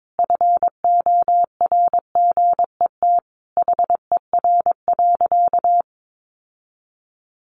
Morse code, heard as a keyed tone, about 22 wpm, 700 Hz. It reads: FORGET HER.
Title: Sentences from Top 1500 Words 22wpm